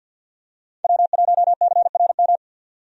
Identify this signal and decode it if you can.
Morse code, heard as a keyed tone, about 51 wpm, 700 Hz. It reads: W1CRM
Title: USA Call Signs - 5 Characters - 50wpm